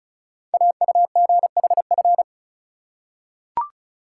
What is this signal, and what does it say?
Morse code, heard as a keyed tone, about 36 wpm, 700 Hz. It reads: AUGHF
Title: Code Groups - 5 Letters - 35wpm